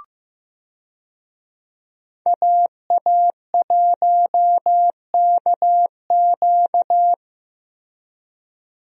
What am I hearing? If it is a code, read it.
Morse code, heard as a keyed tone, about 15 wpm, 700 Hz. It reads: AA1KQ
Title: USA Call Signs - 5 Characters - Repeated 3x - 15wpm